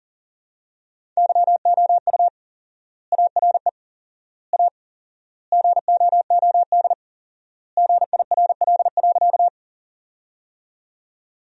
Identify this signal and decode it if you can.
Morse code, heard as a keyed tone, about 40 wpm, 700 Hz. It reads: YOU ARE A GOOD GIRL.
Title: 4x Word-Spacing - Sentences from Top 400 Words - 40wpm